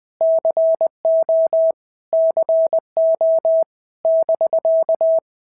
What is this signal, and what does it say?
Morse code, heard as a keyed tone, about 20 wpm, 650 Hz. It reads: CO CO <BK>
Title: POTA QSO Practice - 20wpm